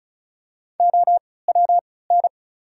Morse code, heard as a keyed tone, about 36 wpm, spaced as for 20 wpm, 700 Hz. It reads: OWN
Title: 3x Character Spacing - Top 300 Words - 35wpm